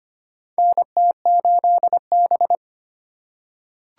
Morse code, heard as a keyed tone, about 25 wpm, 700 Hz. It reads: NT8B
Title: USA Call Signs - 4 Characters - 3x Character Spacing - 25wpm